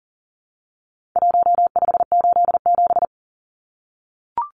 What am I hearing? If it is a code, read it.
Morse code, heard as a keyed tone, about 40 wpm, 700 Hz. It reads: 1587
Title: Numbers - 4-Digit 40wpm